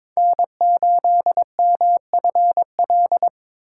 Morse code, heard as a keyed tone, about 22 wpm, 700 Hz. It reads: N8MFL
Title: USA Call Signs - 5 Characters - 2x Character Spacing - 22wpm